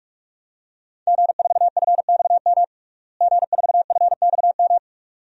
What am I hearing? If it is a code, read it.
Morse code, heard as a keyed tone, about 46 wpm, 700 Hz. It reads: G4FXM G4FXM
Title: DXCC-100 Call Signs - 5 Characters - Repeated 2x - 45wpm